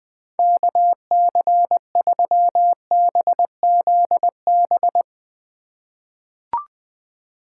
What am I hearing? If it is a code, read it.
Morse code, heard as a keyed tone, about 20 wpm, 700 Hz. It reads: KC3BZB
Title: USA Call Signs - 6 Characters - 3x Character Spacing - 20wpm